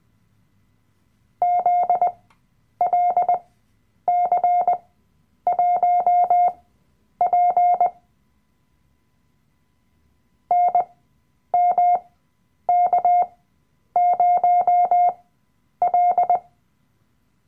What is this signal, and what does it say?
Morse code, heard as a keyed tone, about 20 wpm, spaced as for 9 wpm, 710 Hz. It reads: ZLC1P NMK0L
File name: Workshop Dateien / CW-Online-Kurs / Lektion 007